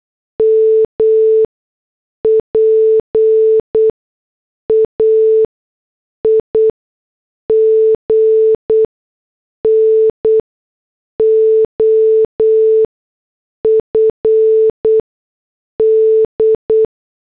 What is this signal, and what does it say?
Morse code, heard as a keyed tone, about 8 wpm, spaced as for 6 wpm, 440 Hz. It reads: MPAIGN OF D
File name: morse / headlines